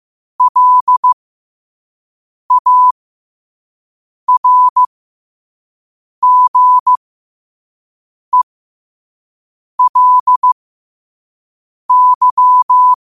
Morse code, heard as a keyed tone, about 15 wpm, spaced as for 5 wpm, 1000 Hz. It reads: LARGELY
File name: 5wpm